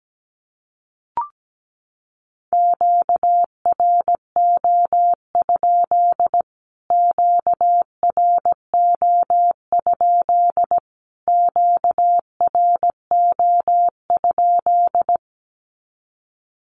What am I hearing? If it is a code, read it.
Morse code, heard as a keyed tone, about 17 wpm, 700 Hz. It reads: QRO? QRO? QRO?
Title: Q-Codes Repeated 3x 17wpm